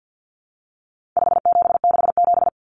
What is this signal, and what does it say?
Morse code, heard as a keyed tone, about 48 wpm, 700 Hz. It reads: 5767
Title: Common Cut Numbers - 4-Digit 50wpm